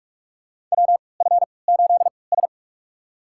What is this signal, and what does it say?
Morse code, heard as a keyed tone, about 46 wpm, spaced as for 26 wpm, 700 Hz. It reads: WF8S
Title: USA Call Signs - 4 Characters - 3x Character Spacing - 45wpm